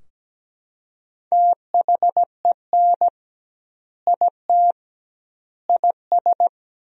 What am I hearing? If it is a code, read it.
Morse code, heard as a keyed tone, about 17 wpm, 700 Hz. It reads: THEN IT IS
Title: 2x Word-Spacing - Sentences from Top 500 Words - 17wpm